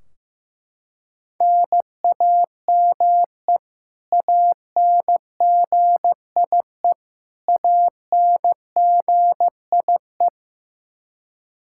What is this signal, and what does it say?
Morse code, heard as a keyed tone, about 15 wpm, 700 Hz. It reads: NAME ANGIE ANGIE
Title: K1USN SST QSO Practice 15wpm